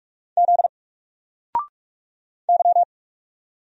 Morse code, heard as a keyed tone, about 44 wpm, 700 Hz. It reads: Z Y
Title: Single Letter-Number Alt 45wpm